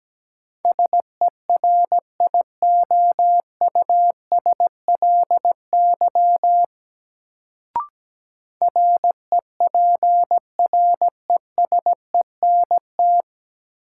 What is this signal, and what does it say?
Morse code, heard as a keyed tone, about 17 wpm, 700 Hz. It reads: SERIOUSLY REPRESENT
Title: Sentences from Top 1200 Words - New Word Drill Alt - 17wpm